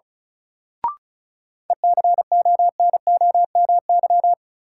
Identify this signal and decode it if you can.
Morse code, heard as a keyed tone, about 36 wpm, 700 Hz. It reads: ECONOMY
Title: Sentences from Top 1000 Words - Review Part 3 - New Word Drill - 35wpm